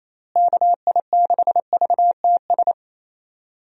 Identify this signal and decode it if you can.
Morse code, heard as a keyed tone, about 27 wpm, 700 Hz. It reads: KI6VTS
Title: USA Call Signs - 6 Characters - Repeated 2x - 28wpm